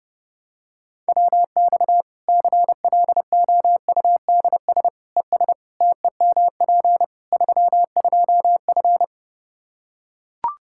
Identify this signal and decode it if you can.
Morse code, heard as a keyed tone, about 30 wpm, 700 Hz. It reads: WX CLOUDS ES TEMP 32F
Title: Practice QSO - Exchange 2 - Info - 30wpm